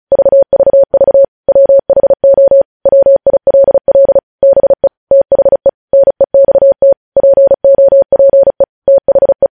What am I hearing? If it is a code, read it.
Morse code, heard as a keyed tone, about 36 wpm, 550 Hz. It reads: VVV WHO WILL BE THE NEXT POPE THE
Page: news headlines in morse code at 35 wpm